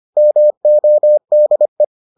Morse code, heard as a keyed tone, about 25 wpm, 600 Hz. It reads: MODE